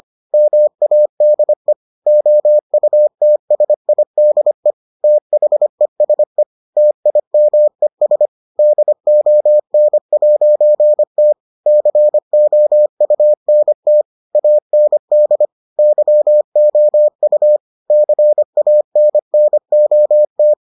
Morse code, heard as a keyed tone, about 25 wpm, 600 Hz. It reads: MADE OUTSIDE THESE TIMES DON'T COUNT AND YOU CANNOT